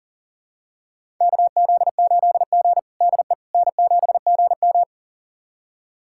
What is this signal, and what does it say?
Morse code, heard as a keyed tone, about 37 wpm, 700 Hz. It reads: KZ8G DE N7GM
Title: USA Callsign DE Callsign - 4 Characters - 40wpm